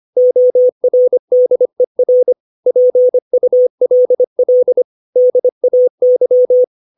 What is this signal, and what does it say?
Morse code, heard as a keyed tone, about 25 wpm, 500 Hz. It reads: ORDER PULL DAY
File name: 632 words / 632 Words 25WPM